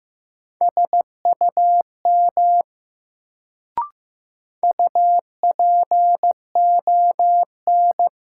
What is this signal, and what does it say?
Morse code, heard as a keyed tone, about 15 wpm, 700 Hz. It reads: SUM UPON
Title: Sentences from Top 1000 Words - Review Part 6 - New Word Drill - 15wpm